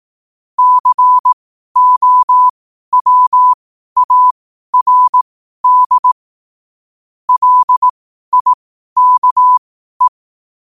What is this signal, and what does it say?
Morse code, heard as a keyed tone, about 18 wpm, spaced as for 12 wpm, 1000 Hz. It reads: COWARD LIKE
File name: qotdfiles / 12wpm